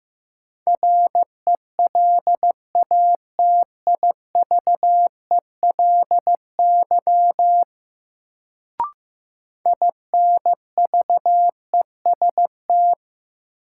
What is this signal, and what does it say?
Morse code, heard as a keyed tone, about 15 wpm, 700 Hz. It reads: RELATIVELY INVEST
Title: Sentences from Top 1100 Words - New Word Drill Alt - 15wpm